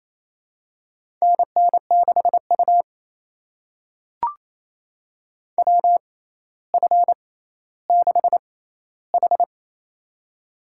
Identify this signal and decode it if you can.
Morse code, heard as a keyed tone, about 28 wpm, spaced as for 14 wpm, 700 Hz. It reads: NN6U WF6H
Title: USA Call Signs - 4 Characters - 6x Character Spacing - 28wpm